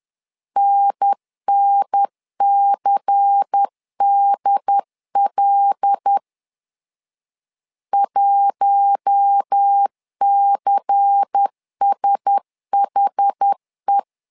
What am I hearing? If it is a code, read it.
Morse code, heard as a keyed tone, about 10 wpm, 800 Hz. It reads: NNCDL 1CSHE
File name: LEZIONI / 5 LBVC / 5 LBVC 10wpm 2349